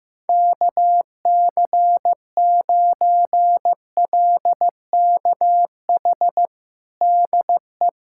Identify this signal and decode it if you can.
Morse code, heard as a keyed tone, about 15 wpm, 700 Hz. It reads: KC9LKH DE
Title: USA Callsign DE Callsign 15wpm